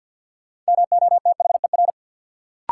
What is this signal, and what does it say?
Morse code, heard as a keyed tone, about 49 wpm, 700 Hz. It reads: MOTHER
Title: Sentences from Top 200 Words - New Word Drill - 50wpm